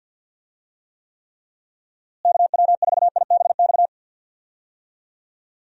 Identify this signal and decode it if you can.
Morse code, heard as a keyed tone, about 52 wpm, 700 Hz. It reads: KW4IBX
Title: USA Call Signs - 6 Characters - Repeated 3x - 50wpm